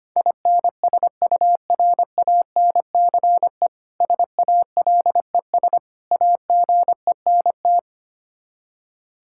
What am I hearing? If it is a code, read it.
Morse code, heard as a keyed tone, about 25 wpm, 700 Hz. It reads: INSURANCE SALES AGENT
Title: Practice QSO - Exchange 3 - More Info - 25wpm